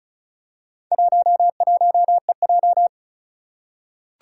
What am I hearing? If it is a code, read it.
Morse code, heard as a keyed tone, about 34 wpm, 700 Hz. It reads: 11EJ
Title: Four Challenging Letter-Number Combinations 35wpm